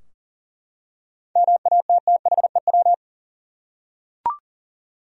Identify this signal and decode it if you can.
Morse code, heard as a keyed tone, about 40 wpm, 700 Hz. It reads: MATTHEW
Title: USA Common Names - 40wpm